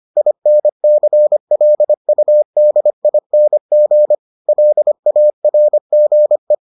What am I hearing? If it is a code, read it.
Morse code, heard as a keyed tone, about 25 wpm, 600 Hz. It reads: INCLUDING LARGE